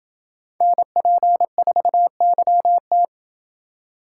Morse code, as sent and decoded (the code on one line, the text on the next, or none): -. .--. ....- -.-- -
NP4YT